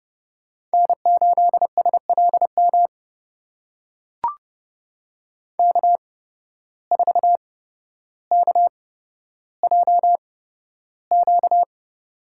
Morse code, heard as a keyed tone, about 30 wpm, spaced as for 13 wpm, 700 Hz. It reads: N8SLM K4KJQ
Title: USA Call Signs - 5 Characters - 8x Character Spacing - 30wpm